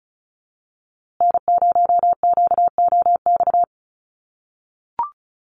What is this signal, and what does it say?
Morse code, heard as a keyed tone, about 34 wpm, 700 Hz. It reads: N0QOX
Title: USA Call Signs Repeated 3x 35wpm